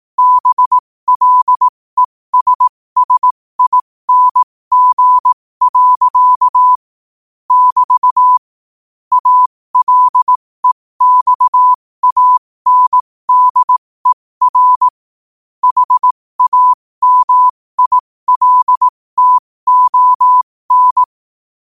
Morse code, heard as a keyed tone, about 18 wpm, spaced as for 15 wpm, 1000 Hz. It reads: BLESSING. <BT> ALEXANDER HAMILTON
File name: qotdfiles / 15wpm